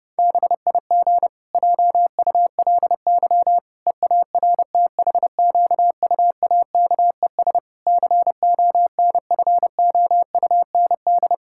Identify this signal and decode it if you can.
Morse code, heard as a keyed tone, about 28 wpm, 700 Hz. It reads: BIG JULY EARTHQUAKES CONFOUND
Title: Pangram Challenge v2 - 30wpm